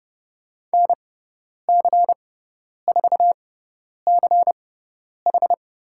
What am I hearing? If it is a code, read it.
Morse code, heard as a keyed tone, about 30 wpm, spaced as for 10 wpm, 700 Hz. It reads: NC4CH
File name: Lesson 17g - Letter u - Call Signs 30wpm